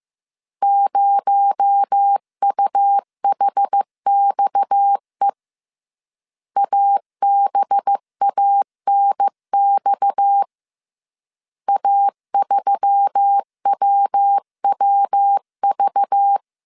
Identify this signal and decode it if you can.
Morse code, heard as a keyed tone, about 15 wpm, 800 Hz. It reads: 0UHXE ABANX A3WWV
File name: CW / DET / 1 6 1500C S C 14wpm 2951